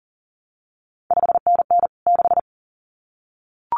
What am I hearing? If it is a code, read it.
Morse code, heard as a keyed tone, about 40 wpm, 700 Hz. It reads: 5NN 6
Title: CQ WW DX Contest Exchange Encore - 40wpm